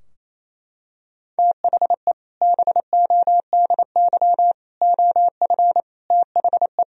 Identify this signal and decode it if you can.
Morse code, heard as a keyed tone, about 27 wpm, 700 Hz. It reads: THE BODY OF THE
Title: Sentences from Top 1000 Words - Review Part 2 - 28wpm